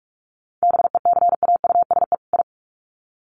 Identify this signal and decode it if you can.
Morse code, heard as a keyed tone, about 44 wpm, 700 Hz. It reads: BECAUSE I
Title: Sets of 2 Words 45wpm